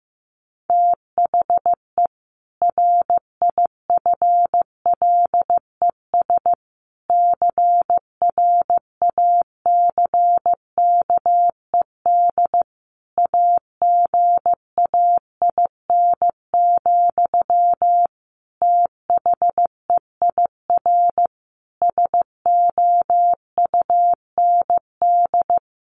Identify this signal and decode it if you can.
Morse code, heard as a keyed tone, about 15 wpm, 700 Hz. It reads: THE RIFLES CRACKED AGAIN, THEIR SOUND
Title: A Sound Of Thunder 15wpm